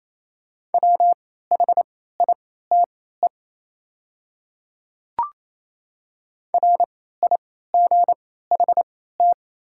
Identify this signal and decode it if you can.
Morse code, heard as a keyed tone, about 28 wpm, spaced as for 16 wpm, 700 Hz. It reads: WHITE RIGHT
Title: 3x Character Spacing - Top 300 Words - 28wpm